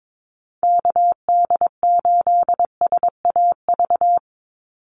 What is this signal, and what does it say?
Morse code, heard as a keyed tone, about 22 wpm, 700 Hz. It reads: KD8SAV
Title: Call Signs - 22wpm